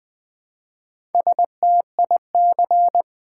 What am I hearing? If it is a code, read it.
Morse code, heard as a keyed tone, about 20 wpm, 700 Hz. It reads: STIC
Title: Build A Word - 1 Letter At A Time - 3 to 5-Letter Words - 20wpm